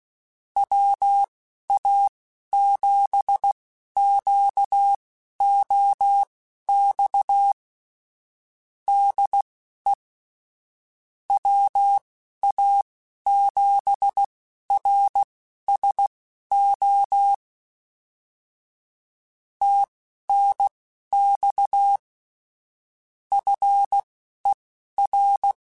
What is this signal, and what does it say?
Morse code, heard as a keyed tone, about 16 wpm, spaced as for 11 wpm, 800 Hz. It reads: WA7QOX DE WA7RSO TNX FER
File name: QSOs / QSOs-mp3s / WA7QOX 2@10